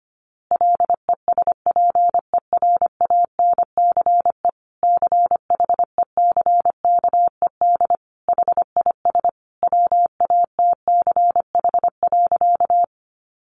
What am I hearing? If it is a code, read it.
Morse code, heard as a keyed tone, about 25 wpm, 700 Hz. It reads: LESPERANCE CHECKED HIS WATCH.
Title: A Sound Of Thunder 25wpm